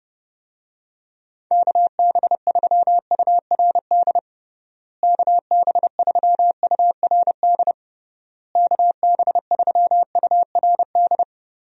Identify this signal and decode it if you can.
Morse code, heard as a keyed tone, about 30 wpm, 700 Hz. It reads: KB3URD KB3URD KB3URD
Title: USA Call Signs - 6 Characters - Repeated 3x - 30wpm